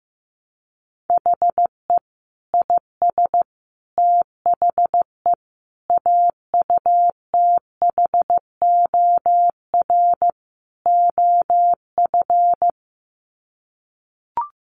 Morse code, heard as a keyed tone, about 15 wpm, 700 Hz. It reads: HE IS THE AUTHOR OF
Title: Sets of 5 Words Encore2 15wpm